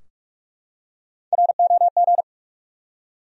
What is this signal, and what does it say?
Morse code, heard as a keyed tone, about 42 wpm, 700 Hz. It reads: ROG
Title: USA Common Names - Max 3 Letters - 45wpm